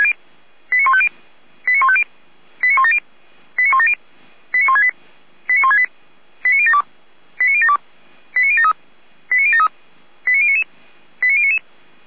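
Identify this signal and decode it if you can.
Five-tone selective call (ZVEI), heard as quick runs of five tones, tones of about 70 ms. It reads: 89150 89150 89170 89170 89168 89168 89072 89072 89073 89073 89099 89099
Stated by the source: ZVEI five-tone selective call received by SDR.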